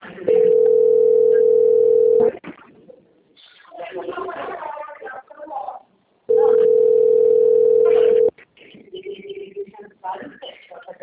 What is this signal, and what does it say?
Ringback tone (North America), 440 + 480 Hz, with speech.
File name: out- - - - - .2931